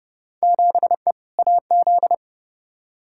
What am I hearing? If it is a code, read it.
Morse code, heard as a keyed tone, about 30 wpm, 700 Hz. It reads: 7E AZ
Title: Field Day Exchanges Encore - 30wpm